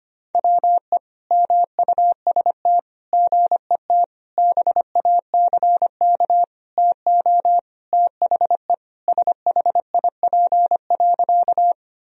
Morse code, heard as a keyed tone, about 25 wpm, 700 Hz. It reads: WE MUST GET BACK TO THE SHIP.
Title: Sentences from Top 1000 Words - Review Part 2 - 25wpm